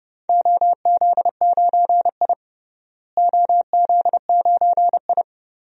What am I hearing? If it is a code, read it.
Morse code, heard as a keyed tone, about 31 wpm, 700 Hz. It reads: OZ9I OZ9I
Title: DXCC-100 Call Signs - 4 Characters - Repeated 2x - 30wpm